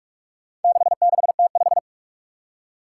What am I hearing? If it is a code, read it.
Morse code, heard as a keyed tone, about 41 wpm, 700 Hz. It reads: 66T5
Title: Four Challenging Letter-Number Combinations 45wpm